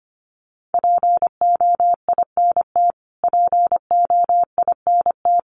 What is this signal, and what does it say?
Morse code, heard as a keyed tone, about 25 wpm, 700 Hz. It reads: POINT POINT
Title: Top 500 Words Repeated 25wpm